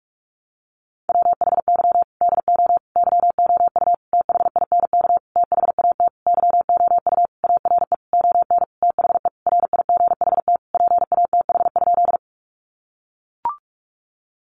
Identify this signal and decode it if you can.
Morse code, heard as a keyed tone, about 44 wpm, 700 Hz. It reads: WHY DO YOU THINK THAT YOU ARE ON THE RIGHT PATH?
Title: Sentences from Top 1400 Words 45wpm